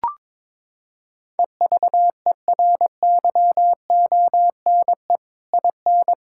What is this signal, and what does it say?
Morse code, heard as a keyed tone, about 22 wpm, 700 Hz. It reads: EVERYONE IN
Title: Sentences from Top 1600 Words 22wpm